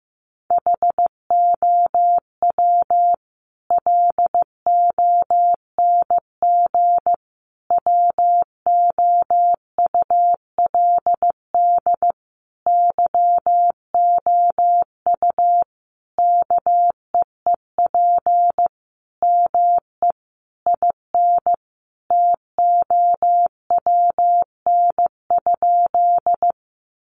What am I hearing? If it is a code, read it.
Morse code, heard as a keyed tone, about 15 wpm, 700 Hz. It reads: HOW LONG WOULD YOU KEEP ME IN TOWN?
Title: Sentences from Top 500 Words 15wpm